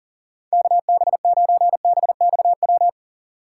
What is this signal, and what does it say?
Morse code, heard as a keyed tone, about 40 wpm, 700 Hz. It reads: KB9BXW
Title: USA Call Signs - 2x Character Spacing - 40wpm